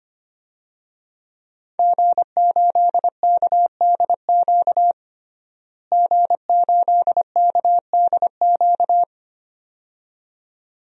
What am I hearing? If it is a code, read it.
Morse code, heard as a keyed tone, about 25 wpm, 700 Hz. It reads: G8KDQ G8KDQ
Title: DXCC-100 Call Signs - 5 Characters - Repeated 2x - 25wpm